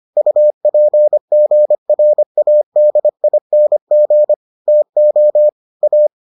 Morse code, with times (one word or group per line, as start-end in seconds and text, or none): 0.2-4.3 UPGRADING
4.7-5.5 TO
5.8-6.1 A